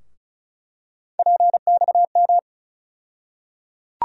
Morse code, heard as a keyed tone, about 34 wpm, 700 Hz. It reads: PXM
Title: USA Call Sign Postfixes - 3 Letters 35wpm